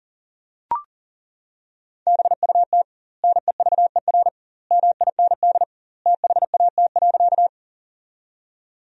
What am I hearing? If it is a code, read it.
Morse code, heard as a keyed tone, about 40 wpm, 700 Hz. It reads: BUT NEVER MIND THAT.
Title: 2x Word-Spacing - Sentences from Top 500 Words - 40wpm